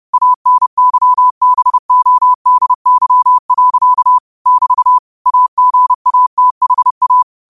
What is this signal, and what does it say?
Morse code, heard as a keyed tone, about 30 wpm, 1000 Hz. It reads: ANYBODY. <BT> AGATHA
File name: qotdfiles / 30wpm